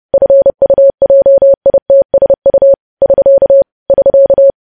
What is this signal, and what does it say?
Morse code, heard as a keyed tone, about 30 wpm, 550 Hz. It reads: FUJITSU <SK> <SK>
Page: news headlines in morse code at 30 wpm